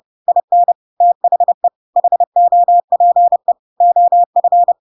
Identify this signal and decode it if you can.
Morse code, heard as a keyed tone, about 30 wpm, 700 Hz. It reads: IN THE HOPE OF
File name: Pinocchio / Pinocchio ch5 30 30 ik1hge